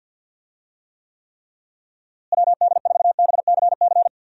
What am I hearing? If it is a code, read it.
Morse code, heard as a keyed tone, about 46 wpm, 700 Hz. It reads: WD4BCC
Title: USA Call Signs - 6 Characters - Repeated 3x - 50wpm